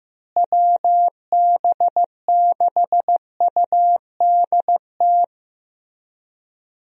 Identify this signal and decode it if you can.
Morse code, heard as a keyed tone, about 15 wpm, 700 Hz. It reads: WB6UDT
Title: USA Call Signs - 6 Characters - Repeated 3x - 15wpm